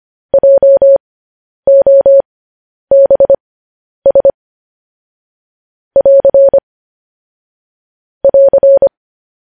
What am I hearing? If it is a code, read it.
Morse code, heard as a keyed tone, about 25 wpm, spaced as for 10 wpm, 550 Hz. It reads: JOBS <AR> <AR>
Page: news-headlines-in-morse-code-at-10-wpm